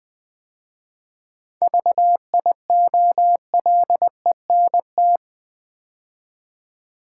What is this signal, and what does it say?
Morse code, heard as a keyed tone, about 20 wpm, 700 Hz. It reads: VIOLENT
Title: 7 Letter Words - First 3 Letters Given - 20wpm